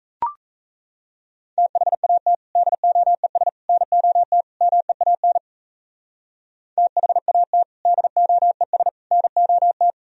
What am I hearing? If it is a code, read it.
Morse code, heard as a keyed tone, about 35 wpm, 700 Hz. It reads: THAT DOES NOT MEAN THAT DOES NOT
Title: Speed-Racing - Sets of 4 Words 28wpm